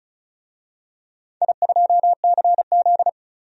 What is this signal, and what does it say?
Morse code, heard as a keyed tone, about 36 wpm, 700 Hz. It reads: I2CZ
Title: DXCC-100 Call Signs - 4 Characters - 35wpm